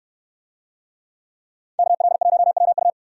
Morse code, heard as a keyed tone, about 66 wpm, 700 Hz. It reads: KL1WU